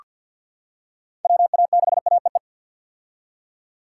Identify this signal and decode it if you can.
Morse code, heard as a keyed tone, about 48 wpm, 700 Hz. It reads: WA6AEE
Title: K1USN SST QSO Practice - 50wpm